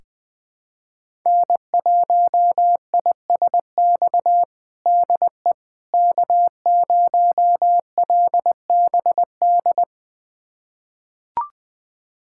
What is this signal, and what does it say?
Morse code, heard as a keyed tone, about 21 wpm, 700 Hz. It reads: N1ISX DE K0LBD
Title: USA Callsign DE Callsign - 5 Characters - 20wpm